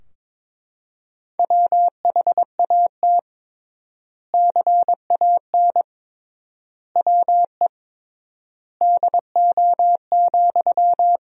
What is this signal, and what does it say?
Morse code, heard as a keyed tone, about 22 wpm, 700 Hz. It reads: WHAT CAN WE DO,
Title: 3x Word-Spacing - Sentences from Top 100 Words - 22wpm